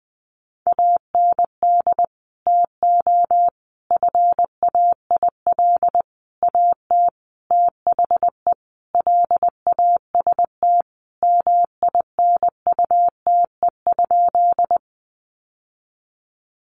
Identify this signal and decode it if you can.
Morse code, heard as a keyed tone, about 20 wpm, 700 Hz. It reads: AND TO FAIL AT THE LAST MINUTE?
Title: Sentences from Top 1100 Words 20wpm